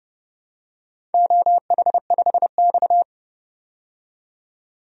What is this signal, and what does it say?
Morse code, heard as a keyed tone, about 30 wpm, 700 Hz. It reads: OH5X
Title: DXCC-100 Call Signs - 4 Characters - 30wpm